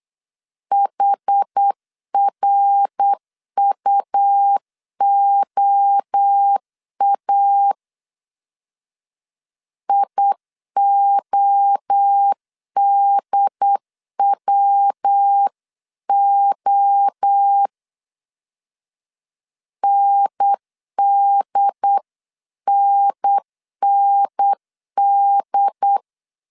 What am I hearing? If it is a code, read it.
Morse code, heard as a keyed tone, about 8 wpm, 800 Hz. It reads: HRUOA IODWO NDNND